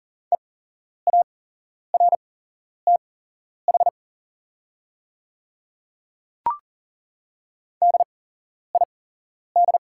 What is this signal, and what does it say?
Morse code, heard as a keyed tone, about 40 wpm, spaced as for 11 wpm, 690 Hz. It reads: EARTH DID
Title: 8x Character Spacing - Top 300 Words - 40wpm